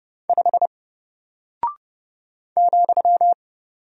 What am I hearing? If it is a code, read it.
Morse code, heard as a keyed tone, about 29 wpm, 700 Hz. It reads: 5 ,
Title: Single Character (Letter:Number:Punctuation:Prosign) Alt 30wpm